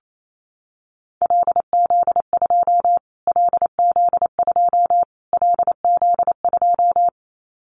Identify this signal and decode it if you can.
Morse code, heard as a keyed tone, about 28 wpm, 700 Hz. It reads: LZ2 LZ2 LZ2
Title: Callsign Prefixes for Top 25 DXCC - Repeated 3x - 28wpm